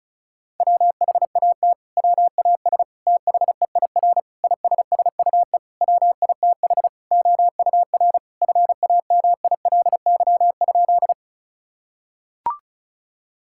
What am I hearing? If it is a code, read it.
Morse code, heard as a keyed tone, about 36 wpm, 700 Hz. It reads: WHAT WAS THEIR ISSUE WITH OUR FAMILY?
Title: Sentences from Top 1000 Words - Review Part 6 - 35wpm